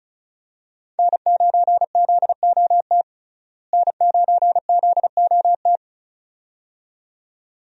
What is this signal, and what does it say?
Morse code, heard as a keyed tone, about 36 wpm, 700 Hz. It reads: N9ZOT N9ZOT
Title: USA Call Signs - 5 Characters - Repeated 2x - 35wpm